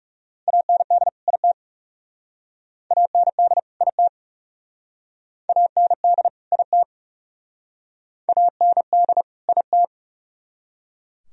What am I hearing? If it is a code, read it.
Morse code, heard as a keyed tone, about 38 wpm, 700 Hz. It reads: AND IT AND IT AND IT AND IT